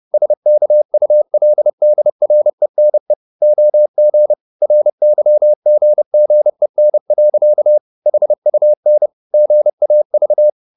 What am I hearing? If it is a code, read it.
Morse code, heard as a keyed tone, about 30 wpm, 600 Hz. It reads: SKULDRENE OG RYGGEN. HUN GAV
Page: Ferie 30 wpm